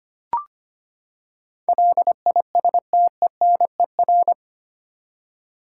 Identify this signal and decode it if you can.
Morse code, heard as a keyed tone, about 25 wpm, 700 Hz. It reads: LISTENER
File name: Eight Letter Words - 25wpm